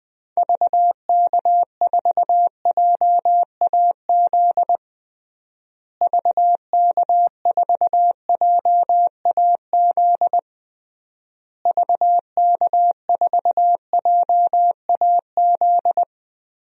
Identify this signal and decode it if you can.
Morse code, heard as a keyed tone, about 20 wpm, 700 Hz. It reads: VK4JAZ VK4JAZ VK4JAZ
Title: DXCC-100 Call Signs - 6 Characters - Repeated 3x - 20wpm